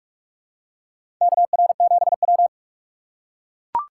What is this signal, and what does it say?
Morse code, heard as a keyed tone, about 43 wpm, 700 Hz. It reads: KR7W
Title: K1USN SST QSO Practice - 45wpm